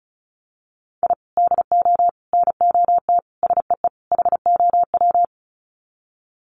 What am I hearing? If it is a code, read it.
Morse code, heard as a keyed tone, about 34 wpm, 700 Hz. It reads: I DO NOT SEE HOW
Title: Sets of 5 Words 35wpm